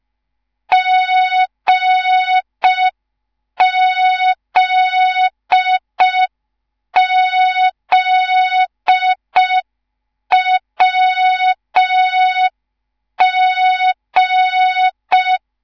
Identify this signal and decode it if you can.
Morse code, heard as a keyed tone, about 5 wpm, 3700 Hz. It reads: GZZWG
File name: Morze-HA2MN / hang / 10GWZ HA2MN